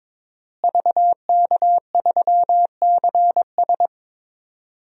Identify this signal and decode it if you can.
Morse code, heard as a keyed tone, about 22 wpm, 700 Hz. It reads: VK3CS
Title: DXCC-100 Call Signs - 5 Characters - Repeated 2x - 22wpm